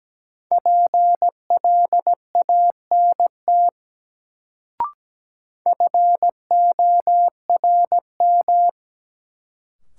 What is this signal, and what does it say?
Morse code, heard as a keyed tone, about 17 wpm, 700 Hz. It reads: PLANT FORM
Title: Sentences from Top 1000 Words - Review Part 1 - New Word Drill - 17wpm